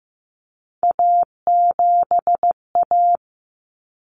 Morse code, heard as a keyed tone, about 15 wpm, 700 Hz. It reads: A7A
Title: Common Cut Numbers - 3-Digit 15wpm